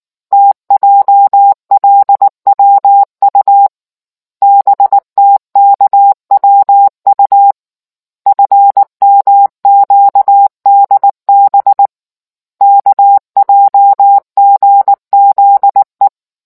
Morse code, heard as a keyed tone, about 19 wpm, 800 Hz. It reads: TJLWU BTKWU FMQDB KJGZE